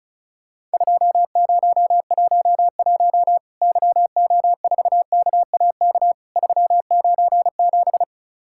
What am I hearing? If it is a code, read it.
Morse code, heard as a keyed tone, about 34 wpm, 700 Hz. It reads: 2011 YO4KAK 397
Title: Worked All Europe DX Contest - QTC Exchange - 35wpm